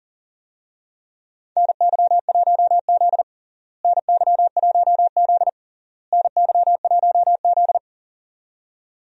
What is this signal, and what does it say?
Morse code, heard as a keyed tone, about 40 wpm, 700 Hz. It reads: NY1Z NY1Z NY1Z
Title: USA Call Signs - 4 Characters - Repeated 3x - 40wpm